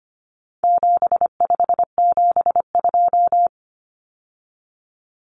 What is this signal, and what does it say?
Morse code, heard as a keyed tone, about 25 wpm, 700 Hz. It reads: 7572
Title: Numbers - 4-Digit 25wpm